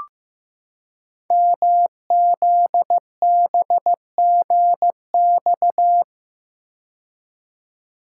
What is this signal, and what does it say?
Morse code, heard as a keyed tone, about 15 wpm, 700 Hz. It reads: MZBGX
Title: Code Groups - 5 Letters - 15wpm